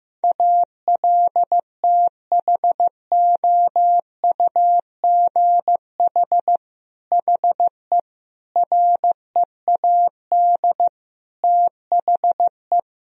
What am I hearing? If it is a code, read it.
Morse code, heard as a keyed tone, about 15 wpm, 700 Hz. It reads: ALTHOUGH HE READ THE
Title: Sentences from Top 1000 Words - Review Part 3 - 15wpm